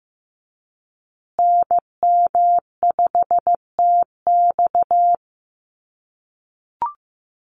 Morse code, heard as a keyed tone, about 15 wpm, 700 Hz. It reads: NM5TX
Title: USA Call Signs Repeated 2x 15wpm